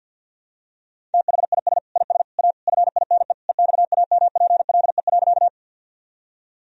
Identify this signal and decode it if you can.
Morse code, heard as a keyed tone, about 52 wpm, 700 Hz. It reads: THIS IS A FINE EXAMPLE.
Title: Sentences from Top 1000 Words - Review Part 1 - 50wpm